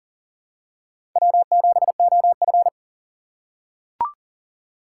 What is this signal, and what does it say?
Morse code, heard as a keyed tone, about 40 wpm, 700 Hz. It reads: W7OF
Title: USA Call Signs - 4 Characters - 6x Character Spacing - 40wpm